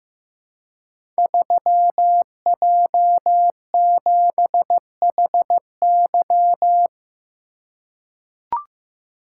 Morse code, heard as a keyed tone, about 15 wpm, 700 Hz. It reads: 3J7HY
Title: Code Groups - 5 Letter:Numbers - 15wpm